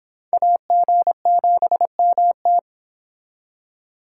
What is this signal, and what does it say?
Morse code, heard as a keyed tone, about 29 wpm, 700 Hz. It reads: AG7MT